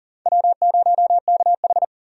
Morse code, heard as a keyed tone, about 40 wpm, 700 Hz. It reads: W0KH
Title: USA Call Signs - 4 Characters - 40wpm